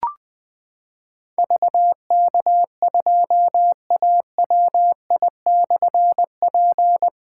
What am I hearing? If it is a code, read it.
Morse code, heard as a keyed tone, about 20 wpm, 700 Hz. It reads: VK2AWI/P
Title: Difficult Call Signs 20wpm